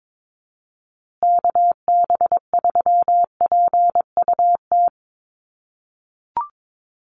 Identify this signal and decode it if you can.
Morse code, heard as a keyed tone, about 22 wpm, 690 Hz. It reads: KB3PUT
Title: USA Call Signs Repeated 3x 22wpm